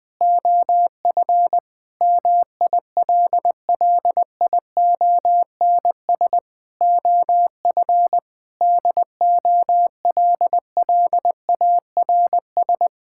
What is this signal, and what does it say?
Morse code, heard as a keyed tone, about 20 wpm, 700 Hz. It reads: OF MILLIONS OF DOLLARS
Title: Sentences with 5-Word Phrases - 20wpm